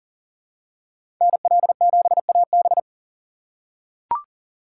Morse code, heard as a keyed tone, about 40 wpm, 700 Hz. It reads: NL7AB
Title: USA Call Signs - 5 Characters - 3x Character Spacing - 40wpm